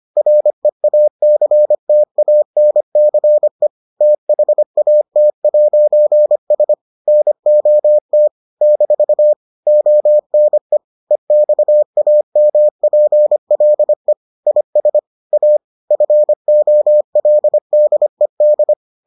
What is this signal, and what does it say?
Morse code, heard as a keyed tone, about 25 wpm, 600 Hz. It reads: REACTANCE THAT'S NOT - ONE EXAMPLE IS A FOLDED